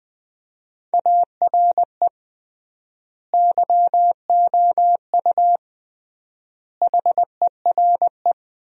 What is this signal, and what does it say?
Morse code, heard as a keyed tone, about 21 wpm, 700 Hz. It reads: ARE YOU HERE
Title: 3x Word-Spacing - Sentences from Top 400 Words - 20wpm